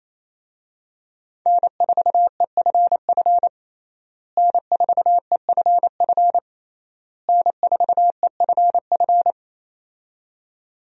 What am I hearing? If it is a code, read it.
Morse code, heard as a keyed tone, about 28 wpm, 700 Hz. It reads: N4EFF N4EFF N4EFF
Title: USA Call Signs - 5 Characters - Repeated 3x - 28wpm